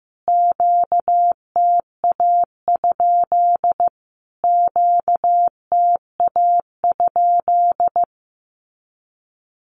Morse code, heard as a keyed tone, about 15 wpm, 700 Hz. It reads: QTA? QTA?
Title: Q-Codes Repeated 3x 15wpm